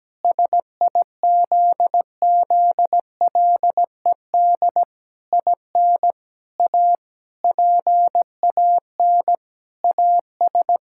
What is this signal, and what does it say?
Morse code, heard as a keyed tone, about 17 wpm, 700 Hz. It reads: SIZZLED IN A PAN AS
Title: Pangram Challenge v2 - 17wpm